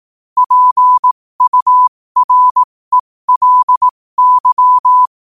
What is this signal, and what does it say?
Morse code, heard as a keyed tone, about 18 wpm, spaced as for 15 wpm, 1000 Hz. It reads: PURELY
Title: 15 WPM morse code quotes for Fri, 01 Mar 2024